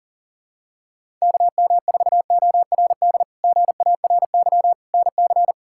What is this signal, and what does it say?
Morse code, heard as a keyed tone, about 40 wpm, 700 Hz. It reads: KM4ORD GARY NC
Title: K1USN SST QSO Practice - 40wpm